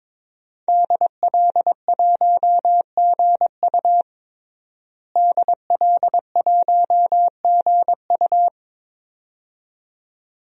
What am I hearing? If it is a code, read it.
Morse code, heard as a keyed tone, about 22 wpm, 700 Hz. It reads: DL1GU DL1GU
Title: DXCC-100 Call Signs - 5 Characters - Repeated 2x - 22wpm